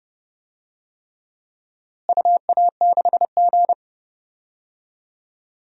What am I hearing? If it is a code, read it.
Morse code, heard as a keyed tone, about 30 wpm, 700 Hz. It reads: UA6G
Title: DXCC-100 Call Signs - 4 Characters - Repeated 3x - 30wpm